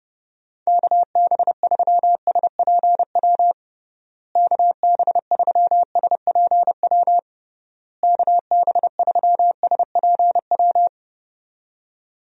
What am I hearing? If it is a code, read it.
Morse code, heard as a keyed tone, about 30 wpm, 700 Hz. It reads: KB3SPW KB3SPW KB3SPW
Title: USA Call Signs - 6 Characters - Repeated 3x - 30wpm